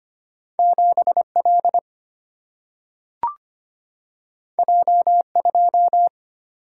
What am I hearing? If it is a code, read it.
Morse code, heard as a keyed tone, about 25 wpm, 700 Hz. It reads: 7 L J 2
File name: Two Letter-Number Combinations - 25wpm